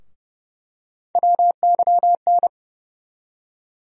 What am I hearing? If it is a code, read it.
Morse code, heard as a keyed tone, about 32 wpm, 700 Hz. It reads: WYN
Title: USA Call Sign Postfixes - 3 Letters 30wpm